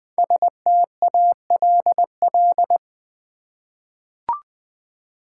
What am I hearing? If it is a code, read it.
Morse code, heard as a keyed tone, about 20 wpm, 700 Hz. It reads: STALL
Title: 5 Letter Words - First 3 Letters Given - 20wpm